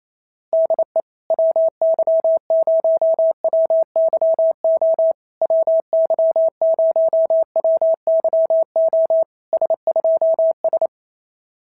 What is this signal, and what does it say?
Morse code, heard as a keyed tone, about 28 wpm, 650 Hz. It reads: DE WY0WYO WY0WYO S2S
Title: SOTA QSO Practice 28wpm